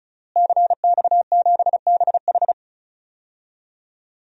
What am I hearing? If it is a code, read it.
Morse code, heard as a keyed tone, about 34 wpm, 700 Hz. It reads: CX7BH
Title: DXCC-100 Call Signs - 5 Characters - Repeated 2x - 35wpm